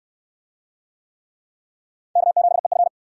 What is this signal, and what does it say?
Morse code, heard as a keyed tone, about 61 wpm, 700 Hz. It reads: K7EF